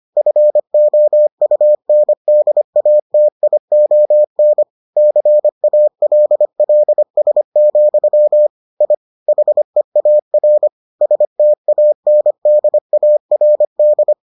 Morse code, heard as a keyed tone, about 25 wpm, 600 Hz. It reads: FOUNDATION CALLS, I HEAR STANDARD